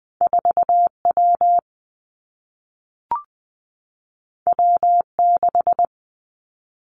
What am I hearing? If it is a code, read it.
Morse code, heard as a keyed tone, about 20 wpm, 700 Hz. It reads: 4 W W 6
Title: Two Challenging Letter-Number Combinations 20wpm